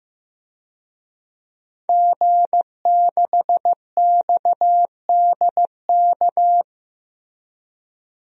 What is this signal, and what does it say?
Morse code, heard as a keyed tone, about 15 wpm, 700 Hz. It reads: G6XDK
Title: DXCC-100 Call Signs - 5 Characters - 15wpm